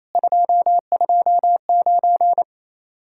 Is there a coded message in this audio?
Yes: Morse code from 0.2 to 2.4 s, heard as a keyed tone, about 35 wpm, 700 Hz. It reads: 229